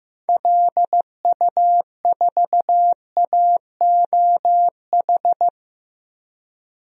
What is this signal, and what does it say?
Morse code, heard as a keyed tone, about 15 wpm, 700 Hz. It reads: LU4AOH
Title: DXCC-100 Call Signs - 6 Characters - 15wpm